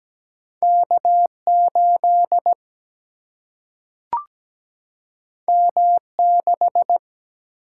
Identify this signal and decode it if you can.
Morse code, heard as a keyed tone, about 17 wpm, 700 Hz. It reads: K 8 M 6
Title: Single-Letter Single-Number 17wpm